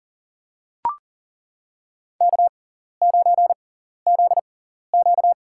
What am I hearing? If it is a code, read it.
Morse code, heard as a keyed tone, about 38 wpm, spaced as for 14 wpm, 700 Hz. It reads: K9ZQ
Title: USA Call Signs - 6x Character Spacing - 40wpm